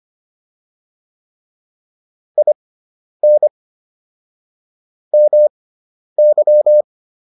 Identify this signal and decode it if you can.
Morse code, heard as a keyed tone, about 25 wpm, spaced as for 10 wpm, 600 Hz. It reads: IN MY